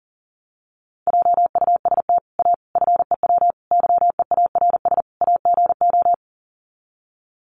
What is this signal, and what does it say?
Morse code, heard as a keyed tone, about 40 wpm, 700 Hz. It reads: JUST A FEW YEARS AGO
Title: Sets of 5 Words Encore 40wpm